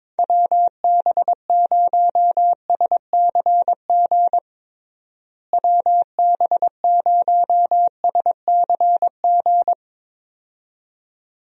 Morse code, heard as a keyed tone, about 22 wpm, 700 Hz. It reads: WB0SCG WB0SCG
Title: USA Call Signs - 6 Characters - Repeated 2x - 22wpm